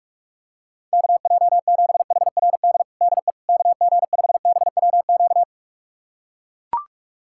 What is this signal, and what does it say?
Morse code, heard as a keyed tone, about 41 wpm, 700 Hz. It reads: KJ7HRD DE KG5BWQ
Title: USA Callsign DE Callsign 45wpm